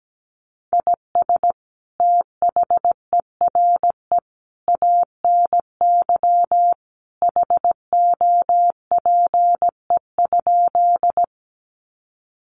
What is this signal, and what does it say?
Morse code, heard as a keyed tone, about 17 wpm, 700 Hz. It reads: IS THERE ANY HOPE?
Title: Sentences from Top 1000 Words 17wpm